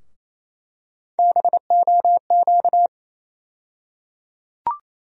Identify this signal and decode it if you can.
Morse code, heard as a keyed tone, about 28 wpm, 700 Hz. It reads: BOQ
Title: USA Call Sign Postfixes - 3 Letters 28wpm